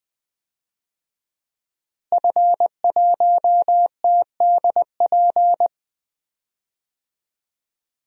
Morse code, heard as a keyed tone, about 20 wpm, 700 Hz. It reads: F1TDP
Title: DXCC-100 Call Signs - 5 Characters - Repeated 2x - 20wpm